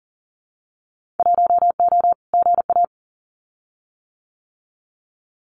Morse code, heard as a keyed tone, about 40 wpm, 700 Hz. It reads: 1O GA
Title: Winter Field Day Exchanges 40wpm